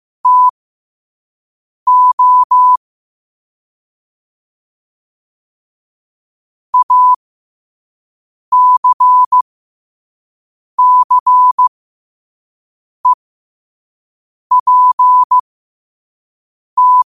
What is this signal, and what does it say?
Morse code, heard as a keyed tone, about 15 wpm, spaced as for 5 wpm, 1000 Hz. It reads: TO ACCEPT
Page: New quotes every day in morse code at 5 Words per minute.